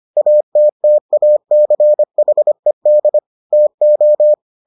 Morse code, heard as a keyed tone, about 25 wpm, 600 Hz. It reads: ATTACHED TO